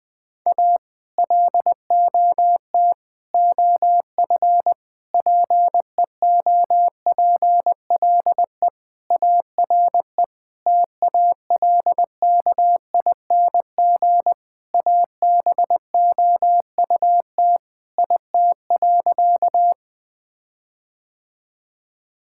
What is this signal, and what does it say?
Morse code, heard as a keyed tone, about 20 wpm, 700 Hz. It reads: A LOT OF PEOPLE ARE TALKING ABOUT IT.
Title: Sentences with 5-Word Phrases - 20wpm